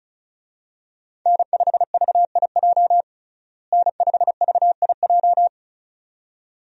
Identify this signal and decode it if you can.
Morse code, heard as a keyed tone, about 36 wpm, 700 Hz. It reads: N5VIJ N5VIJ
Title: USA Call Signs - 5 Characters - Repeated 2x - 35wpm